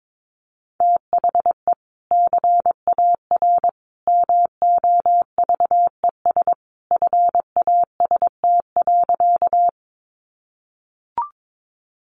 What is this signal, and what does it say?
Morse code, heard as a keyed tone, about 22 wpm, 700 Hz. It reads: THE CAR MOVES FAST.
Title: Easy Phrases 22wpm